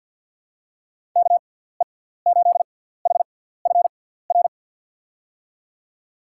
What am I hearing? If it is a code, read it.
Morse code, heard as a keyed tone, about 48 wpm, spaced as for 17 wpm, 700 Hz. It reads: KE8HFR
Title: USA Call Signs - 6 Characters - 6x Character Spacing - 50wpm